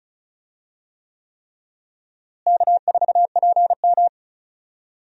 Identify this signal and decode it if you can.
Morse code, heard as a keyed tone, about 36 wpm, 700 Hz. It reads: K4PM
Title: USA Call Signs - 4 Characters - Repeated 3x - 35wpm